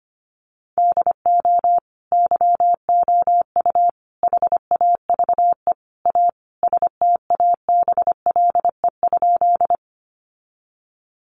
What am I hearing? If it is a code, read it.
Morse code, heard as a keyed tone, about 25 wpm, 700 Hz. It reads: DO YOU HAVE A STABLE?
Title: Sentences from Top 1500 Words 25wpm